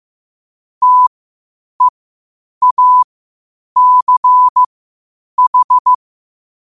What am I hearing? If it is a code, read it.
Morse code, heard as a keyed tone, about 15 wpm, spaced as for 8 wpm, 1000 Hz. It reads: TEACH
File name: qotdfiles / 8wpm